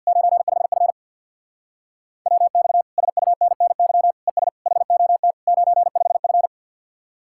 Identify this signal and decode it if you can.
Morse code, heard as a keyed tone, about 67 wpm, spaced as for 58 wpm, 700 Hz. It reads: 95F WX SUNNY ES HOT 95F
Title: Speed-Racing Common QSO Phrases - 50wpm